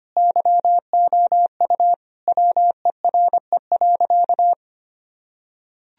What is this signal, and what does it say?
Morse code, heard as a keyed tone, about 25 wpm, 700 Hz. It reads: YOU WERE.
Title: Sentences from Top 1000 Words - Review Part 1 - 25wpm